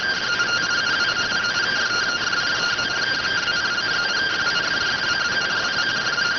Two-tone teleprinter signal, about 170 Hz shift.